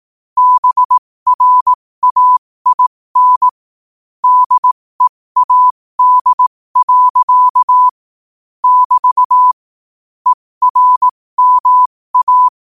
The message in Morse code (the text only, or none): BRAIN DEAD. <BT> ERMA